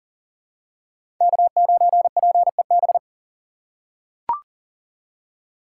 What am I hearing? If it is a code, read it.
Morse code, heard as a keyed tone, about 40 wpm, 700 Hz. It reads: K9PEB
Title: USA Call Signs - 5 Characters - 2x Character Spacing - 40wpm